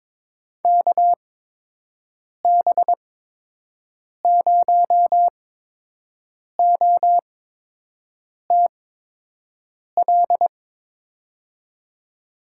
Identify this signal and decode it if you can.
Morse code, heard as a keyed tone, about 22 wpm, spaced as for 6 wpm, 700 Hz. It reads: KB0OTL
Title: USA Call Signs - 8x Character Spacing - 22wpm